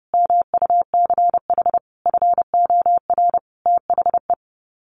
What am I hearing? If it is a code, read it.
Morse code, heard as a keyed tone, about 30 wpm, 700 Hz. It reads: MUCH FOR THE
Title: Sets of 5 Words Encore2 30wpm